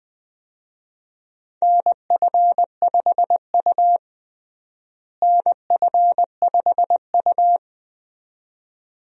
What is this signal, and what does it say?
Morse code, heard as a keyed tone, about 20 wpm, 700 Hz. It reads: NF5U NF5U
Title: USA Call Signs - 4 Characters - Repeated 2x - 20wpm